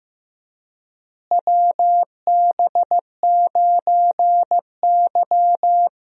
Morse code, heard as a keyed tone, about 15 wpm, 700 Hz. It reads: WB9Y
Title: USA Call Signs - 4 Characters - Repeated 3x - 15wpm